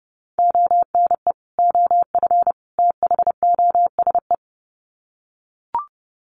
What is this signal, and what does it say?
Morse code, heard as a keyed tone, about 30 wpm, 700 Hz. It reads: ONE OF THOSE
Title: Sets of 3 Words 30wpm